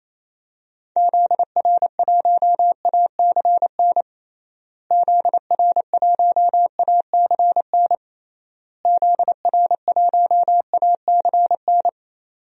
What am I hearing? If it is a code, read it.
Morse code, heard as a keyed tone, about 28 wpm, 700 Hz. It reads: ZR1ACN ZR1ACN ZR1ACN
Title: DXCC-100 Call Signs - 6 Characters - Repeated 3x - 28wpm